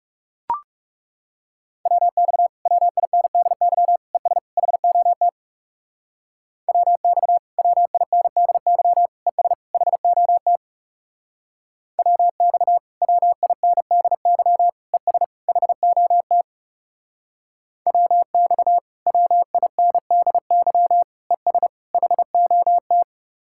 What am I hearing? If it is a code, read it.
Morse code, heard as a keyed tone, about 38 wpm, 700 Hz. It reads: WX WINDY ES HOT WX WINDY ES HOT WX WINDY ES HOT WX WINDY ES HOT